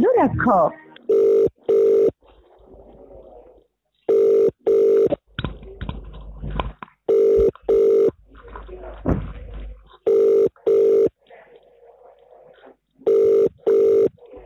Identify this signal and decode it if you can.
Ringback tone (UK), 400 + 450 Hz, with speech.